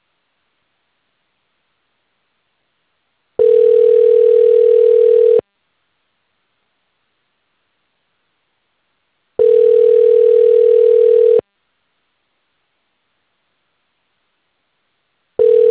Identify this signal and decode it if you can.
Ringback tone (North America), 440 + 480 Hz.